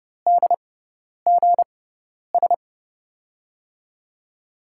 Morse code, heard as a keyed tone, about 30 wpm, spaced as for 10 wpm, 700 Hz. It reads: DGS
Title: USA Call Signs - 6x Character Spacing - 30wpm